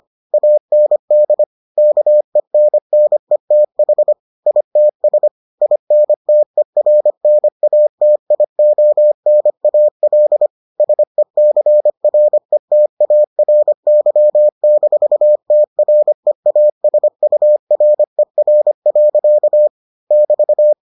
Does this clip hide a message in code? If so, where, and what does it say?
Yes: Morse code from 0.3 to 20.7 s, heard as a keyed tone, about 25 wpm, 600 Hz. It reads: AND KENNETH ITS INTERNATIONAL SECRETARY-TREASURER. <BT>